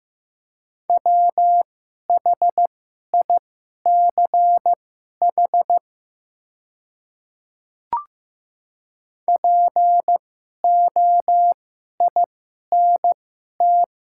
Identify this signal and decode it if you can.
Morse code, heard as a keyed tone, about 15 wpm, spaced as for 11 wpm, 700 Hz. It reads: WHICH POINT
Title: 2x Character Spacing - Top 200 Words - 15wpm